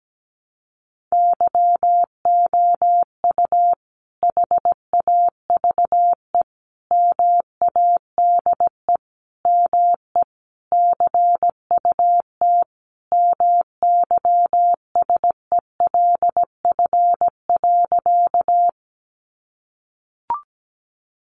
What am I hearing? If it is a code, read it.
Morse code, heard as a keyed tone, about 17 wpm, 700 Hz. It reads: YOU HAVE MADE ME CUT MYSELF.
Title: Sentences from Top 1300 Words 17wpm